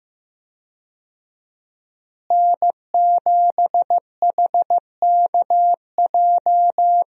Morse code, heard as a keyed tone, about 16 wpm, 700 Hz. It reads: N7HKJ
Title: USA Call Signs - 5 Characters - Repeated 3x - 15wpm